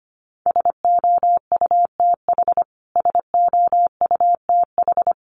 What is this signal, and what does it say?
Morse code, heard as a keyed tone, about 25 wpm, 700 Hz. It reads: SOUTH SOUTH
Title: Top 500 Words Repeated 25wpm